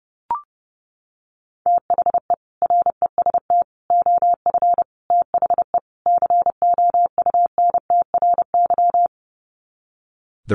Morse code, heard as a keyed tone, about 30 wpm, 700 Hz. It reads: THE REST OF THE COUNTRY
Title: Sets of 5 Words 30wpm